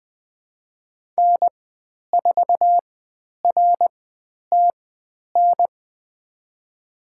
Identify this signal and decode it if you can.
Morse code, heard as a keyed tone, about 20 wpm, spaced as for 10 wpm, 700 Hz. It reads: N4RTN
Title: 095 Lesson 16g - Letter c - Call Signs 20wpm